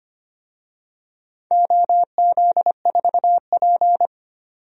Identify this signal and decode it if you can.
Morse code, heard as a keyed tone, about 25 wpm, 700 Hz. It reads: OZ4P
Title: DXCC-100 Call Signs - 4 Characters - 25wpm